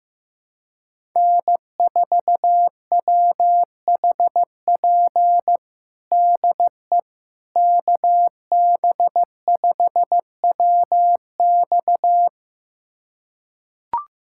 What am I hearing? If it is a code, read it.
Morse code, heard as a keyed tone, about 16 wpm, 700 Hz. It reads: N4WHP DE KB5WX
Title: USA Callsign DE Callsign - 5 Characters - 15wpm